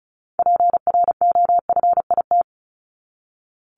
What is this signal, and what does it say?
Morse code, heard as a keyed tone, about 34 wpm, 700 Hz. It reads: PROFIT
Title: Top 751-1000 Words 35wpm